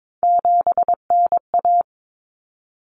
Morse code, heard as a keyed tone, about 22 wpm, 700 Hz. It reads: 7NA
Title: Common Cut Numbers - 3-Digit 22wpm